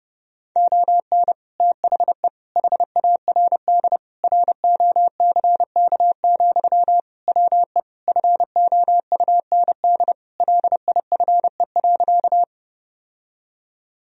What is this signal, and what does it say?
Morse code, heard as a keyed tone, about 32 wpm, 700 Hz. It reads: ON THE HARD ROCK, WE FOUND LIFE.
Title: Sentences from Top 1000 Words - Review Part 2 - 30wpm